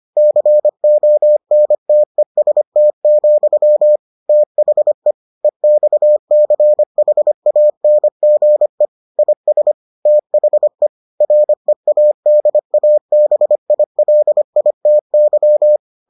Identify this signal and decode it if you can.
Morse code, heard as a keyed tone, about 25 wpm, 600 Hz. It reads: CONTEST, THE EXCHANGE IS THE READABILITY